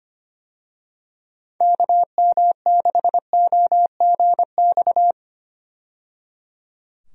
Morse code, heard as a keyed tone, about 25 wpm, 700 Hz. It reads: KM6OGX
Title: USA Call Signs - 6 Characters - 25wpm